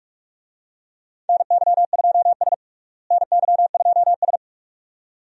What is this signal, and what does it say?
Morse code, heard as a keyed tone, about 47 wpm, 700 Hz. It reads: NY2S NY2S
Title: USA Call Signs - 4 Characters - Repeated 2x - 45wpm